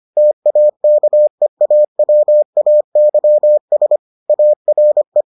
Morse code, heard as a keyed tone, about 25 wpm, 600 Hz. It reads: TAKEAWAYS ARE